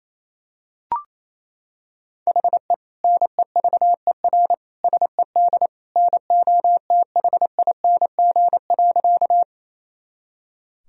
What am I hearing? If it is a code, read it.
Morse code, heard as a keyed tone, about 28 wpm, 700 Hz. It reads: HE NEVER SED NOTHING.
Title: Sentences from Top 500 Words and Abbreviations 28wpm